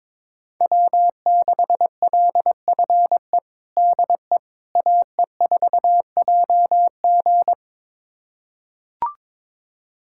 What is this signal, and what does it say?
Morse code, heard as a keyed tone, about 22 wpm, 700 Hz. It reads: W6LFE DE AE4JG
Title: USA Callsign DE Callsign - 5 Characters - 22wpm